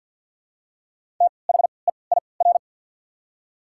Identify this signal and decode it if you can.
Morse code, heard as a keyed tone, about 53 wpm, spaced as for 29 wpm, 700 Hz. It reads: THEIR
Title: 3x Character Spacing - Top 100 Words - 50wpm